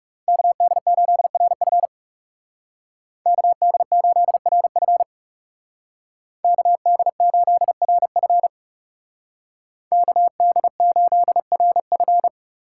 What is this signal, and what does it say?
Morse code, heard as a keyed tone, about 39 wpm, 700 Hz. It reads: KD8RF KD8RF KD8RF KD8RF